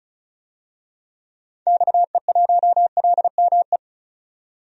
Morse code, heard as a keyed tone, about 34 wpm, 700 Hz. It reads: XE1LME
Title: DXCC-100 Call Signs - 6 Characters - 35wpm